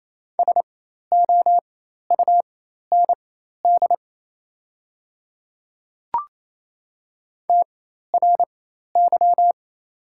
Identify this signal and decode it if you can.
Morse code, heard as a keyed tone, about 28 wpm, spaced as for 13 wpm, 700 Hz. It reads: SOUND TRY
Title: 4x Character Spacing - Top 200 Words - 28wpm